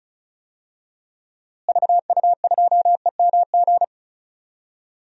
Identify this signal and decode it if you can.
Morse code, heard as a keyed tone, about 36 wpm, 700 Hz. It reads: VU2EMG
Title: DXCC-100 Call Signs - 6 Characters - Repeated 2x - 35wpm